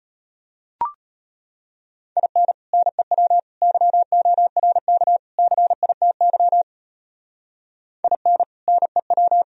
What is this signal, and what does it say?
Morse code, heard as a keyed tone, about 32 wpm, 700 Hz. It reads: IN NEW YORK CITY IN NEW